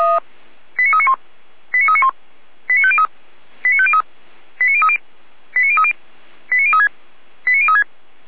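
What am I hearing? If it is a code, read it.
Five-tone selective call (ZVEI), heard as quick runs of five tones, tones of about 70 ms. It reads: 89391 89391 89503 89503 89030 89030 89036 89036
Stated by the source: ZVEI five-tone call received on SDR.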